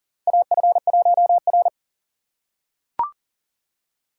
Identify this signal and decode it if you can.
Morse code, heard as a keyed tone, about 37 wpm, 690 Hz. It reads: AF1R
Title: USA Call Signs - 4x Character Spacing - 40wpm